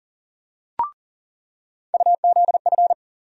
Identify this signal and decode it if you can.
Morse code, heard as a keyed tone, about 40 wpm, 700 Hz. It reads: UZF
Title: USA Call Sign Postfixes - 3 Letters 40wpm